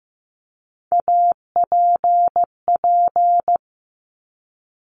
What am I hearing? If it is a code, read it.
Morse code, heard as a keyed tone, about 15 wpm, 700 Hz. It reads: APP
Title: Top Three Letter Combinations in Words 15wpm